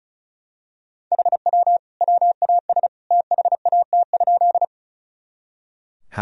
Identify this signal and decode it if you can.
Morse code, heard as a keyed tone, about 36 wpm, 700 Hz. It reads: HW WAS THAT?
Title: Sentences from Top 100 Words and Abbreviations 35wpm